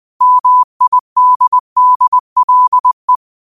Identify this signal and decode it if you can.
Morse code, heard as a keyed tone, about 20 wpm, 1000 Hz. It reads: MIDDLE
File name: qotdfiles / 20wpm